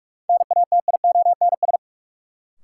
Morse code, heard as a keyed tone, about 44 wpm, 700 Hz. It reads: NATIONS
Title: Sets of 3 Words - New Word Drill - 45wpm